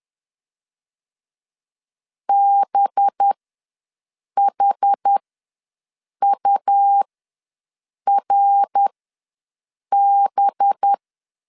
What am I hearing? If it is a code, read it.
Morse code, heard as a keyed tone, about 10 wpm, spaced as for 6 wpm, 800 Hz. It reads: BHURB
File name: LEZIONI / 5 LBVC / 5 LB ws 10wpm 3616